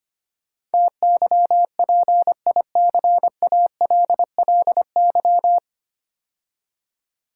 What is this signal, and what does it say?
Morse code, heard as a keyed tone, about 25 wpm, 690 Hz. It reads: TYPICALLY
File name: practice / 25wpm / 9 Letter Words - First 3 Letters Given - 25wpm